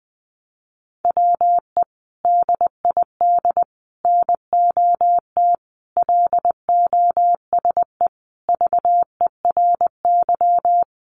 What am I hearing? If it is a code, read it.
Morse code, heard as a keyed tone, about 20 wpm, 700 Hz. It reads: WE DID NOT LOSE VERY
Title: Sentences from Top 1100 Words 20wpm